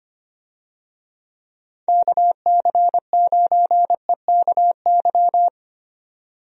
Morse code, heard as a keyed tone, about 25 wpm, 700 Hz. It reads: KC9EKY
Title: USA Call Signs - 6 Characters - 25wpm